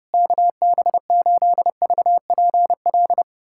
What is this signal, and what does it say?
Morse code, heard as a keyed tone, about 30 wpm, 700 Hz. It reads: KB8VPL
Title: USA Callsign DE Callsign - 6 Characters - 30wpm